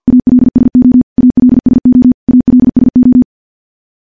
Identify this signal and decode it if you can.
Morse code, heard as a keyed tone, about 50 wpm, 250 Hz. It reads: ALSO ALSO ALSO
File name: morse / morse-training-50WPM